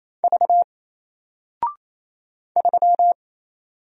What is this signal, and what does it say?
Morse code, heard as a keyed tone, about 27 wpm, 700 Hz. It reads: V 3
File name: Single-Letter-Number-28wpm